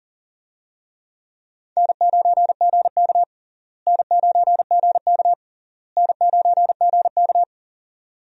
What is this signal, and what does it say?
Morse code, heard as a keyed tone, about 40 wpm, 700 Hz. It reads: N9GK N9GK N9GK
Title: USA Call Signs - 4 Characters - Repeated 3x - 40wpm